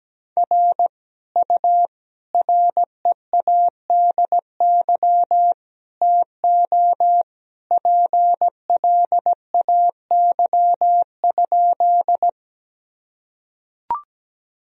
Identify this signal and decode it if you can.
Morse code, heard as a keyed tone, about 17 wpm, 700 Hz. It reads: R U READY TO PLAY?
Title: Sentences from Top 400 Words and Abbreviations 17wpm